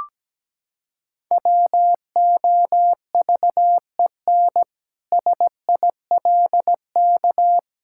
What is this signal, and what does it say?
Morse code, heard as a keyed tone, about 17 wpm, 700 Hz. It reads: WOVEN SILK
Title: Pangram Challenge v2 - 17wpm